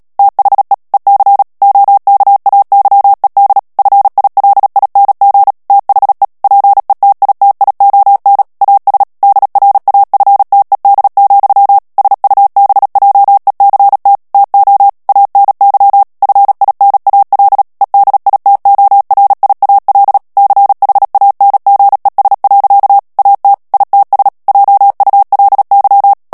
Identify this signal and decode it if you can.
Morse code, heard as a keyed tone, about 37 wpm, 800 Hz. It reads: THE EC OKAYED FILING THE PETITION AS DRAFTED, SUBJECT TO ANY FINAL EDITORIAL CHANGES. AT ITS JULY
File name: morse mp3 / ANGLAIS texte clair 3 vitesse 35 mots minute